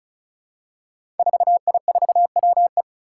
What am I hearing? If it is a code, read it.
Morse code, heard as a keyed tone, about 36 wpm, 700 Hz. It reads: 4I4WE
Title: Five Challenging Letter-Number Combinations 35wpm